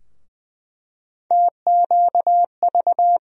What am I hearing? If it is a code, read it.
Morse code, heard as a keyed tone, about 20 wpm, 700 Hz. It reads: TQV
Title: USA Call Sign Postfixes - 3 Letters 20wpm